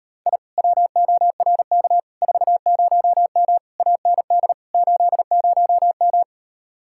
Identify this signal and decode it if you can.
Morse code, heard as a keyed tone, about 33 wpm, 700 Hz. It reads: I WORK 40M AND 80M